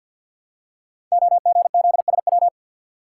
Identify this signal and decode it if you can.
Morse code, heard as a keyed tone, about 48 wpm, 700 Hz. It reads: OGZSW
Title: Code Groups - 5 Letter:Numbers - 50wpm